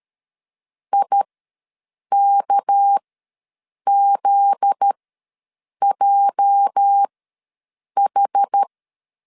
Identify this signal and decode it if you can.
Morse code, heard as a keyed tone, about 13 wpm, spaced as for 7 wpm, 800 Hz. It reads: IKZJH
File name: LEZIONI / 9 2 ws 12wpm 3406